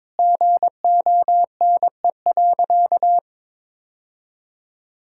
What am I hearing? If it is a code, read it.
Morse code, heard as a keyed tone, about 22 wpm, 700 Hz. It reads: GONE.
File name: practice / 22wpm / Sentences with 3-Word Phrases - 22wpm